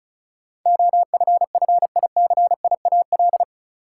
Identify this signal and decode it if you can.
Morse code, heard as a keyed tone, about 36 wpm, 700 Hz. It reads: OFFICIAL
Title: Sentences from Top 1000 Words - New Word Drill - 35wpm